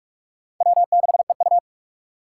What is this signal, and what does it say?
Morse code, heard as a keyed tone, about 41 wpm, 700 Hz. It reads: W6EU
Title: K1USN SST QSO Practice - 45wpm